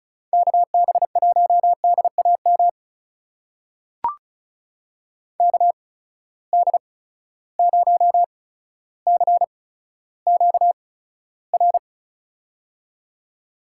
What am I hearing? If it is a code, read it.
Morse code, heard as a keyed tone, about 36 wpm, spaced as for 15 wpm, 700 Hz. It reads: KB1DAM KD0CQR
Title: USA Call Signs - 6 Characters - 8x Character Spacing - 35wpm